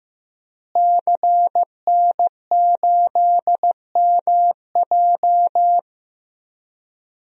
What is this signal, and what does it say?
Morse code, heard as a keyed tone, about 15 wpm, 700 Hz. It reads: CN8MJ
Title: DXCC-100 Call Signs - 5 Characters - Repeated 2x - 15wpm